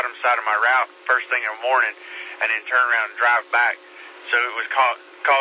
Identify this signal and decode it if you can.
Emergency Alert System (SAME) tones, heard as harsh buzzing bursts, with speech. It is heard from 1.0 to 4.7 s.